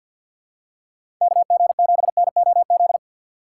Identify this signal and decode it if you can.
Morse code, heard as a keyed tone, about 53 wpm, 700 Hz. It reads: KG7NOZ
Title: USA Call Signs - 6 Characters - 3x Character Spacing - 50wpm